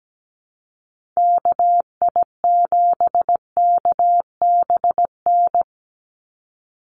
Morse code, heard as a keyed tone, about 17 wpm, 700 Hz. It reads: KI7KBN
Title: USA Call Signs 17wpm